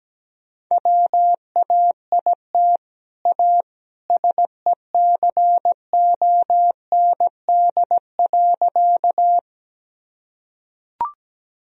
Morse code, heard as a keyed tone, about 17 wpm, 700 Hz. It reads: WAIT A SECOND.
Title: Sentences from Top 1000 Words - Review Part 1 - 17wpm